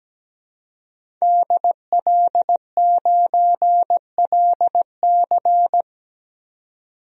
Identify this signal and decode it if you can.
Morse code, heard as a keyed tone, about 17 wpm, 700 Hz. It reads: DL9LC
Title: DXCC-100 Call Signs - 5 Characters - 17wpm